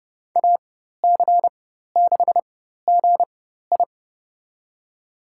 Morse code, heard as a keyed tone, about 30 wpm, spaced as for 14 wpm, 700 Hz. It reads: AC6GI
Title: USA Call Signs - 5 Characters - 4x Character Spacing - 30wpm